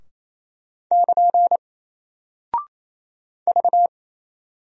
Morse code, heard as a keyed tone, about 27 wpm, 700 Hz. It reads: <KN> V
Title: Single Character (Letter:Number:Punctuation:Prosign) Alt 28wpm